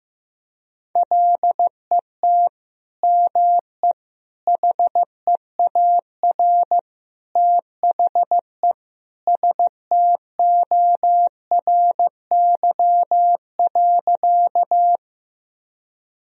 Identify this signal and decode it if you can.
Morse code, heard as a keyed tone, about 15 wpm, 700 Hz. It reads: LET ME HEAR THE STORY.
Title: Sentences from Top 1000 Words - Review Part 1 - 15wpm